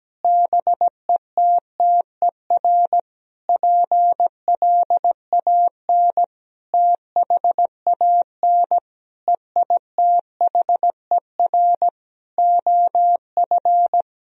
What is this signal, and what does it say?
Morse code, heard as a keyed tone, about 17 wpm, 700 Hz. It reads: BETTER PLAN THAN EITHER OF
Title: Sentences from Top 1000 Words - Review Part 4 - 17wpm